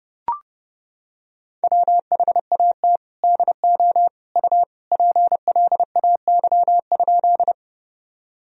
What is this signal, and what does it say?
Morse code, heard as a keyed tone, about 30 wpm, 700 Hz. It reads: WHAT DO U PLAY?
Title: Sentences from Top 200 Words and Abbreviations 30wpm